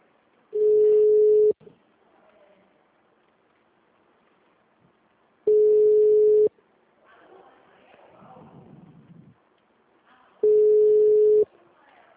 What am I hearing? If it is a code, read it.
Ringback tone (Europe, 425 Hz), with speech.